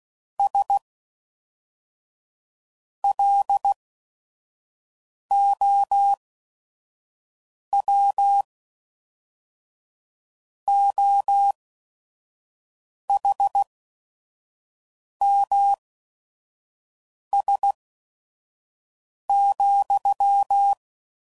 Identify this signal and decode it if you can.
Morse code, heard as a keyed tone, about 16 wpm, spaced as for 5 wpm, 800 Hz. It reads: S LOW OHMS,
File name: CW Code Groups @5WPM / Technical Subjects / XC